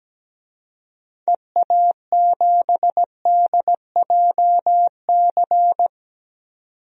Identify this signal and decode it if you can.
Morse code, heard as a keyed tone, about 17 wpm, 700 Hz. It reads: EA7DJC
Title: DXCC-100 Call Signs - 6 Characters - 17wpm